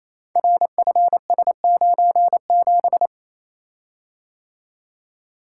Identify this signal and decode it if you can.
Morse code, heard as a keyed tone, about 27 wpm, 700 Hz. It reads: RFS97
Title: Code Groups - 5 Letter:Numbers - 28wpm